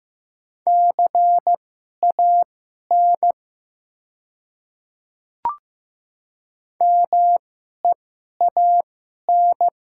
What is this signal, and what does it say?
Morse code, heard as a keyed tone, about 15 wpm, spaced as for 11 wpm, 700 Hz. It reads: CAN MEAN
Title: 2x Character Spacing - Top 200 Words - 15wpm